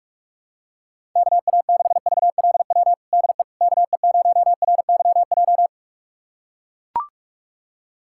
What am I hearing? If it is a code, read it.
Morse code, heard as a keyed tone, about 42 wpm, 700 Hz. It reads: KA6VLW DE KE0RYJ
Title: USA Callsign DE Callsign - 6 Characters - 45wpm